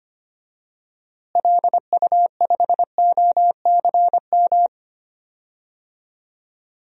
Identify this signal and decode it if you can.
Morse code, heard as a keyed tone, about 25 wpm, 700 Hz. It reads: LU5OCM
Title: DXCC-100 Call Signs - 6 Characters - Repeated 2x - 25wpm